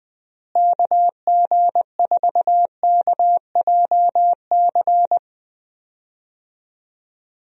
Morse code, heard as a keyed tone, about 21 wpm, 700 Hz. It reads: KG4KJC
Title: USA Call Signs - 6 Characters - Repeated 2x - 20wpm